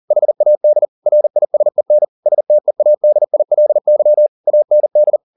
Morse code, heard as a keyed tone, about 40 wpm, 600 Hz. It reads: HAD RISEN STEADILY AND
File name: the old man and the sea 2 40wpm